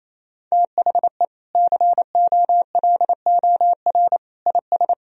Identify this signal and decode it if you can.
Morse code, heard as a keyed tone, about 28 wpm, 700 Hz. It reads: THE COLOR IS
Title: Sentences from Top 1000 Words - Review Part 2 - 28wpm